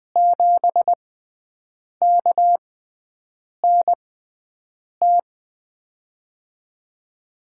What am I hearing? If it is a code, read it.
Morse code, heard as a keyed tone, about 20 wpm, spaced as for 7 wpm, 700 Hz. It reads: 7KNT
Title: USA Call Signs - 6 Characters - 6x Character Spacing - 20wpm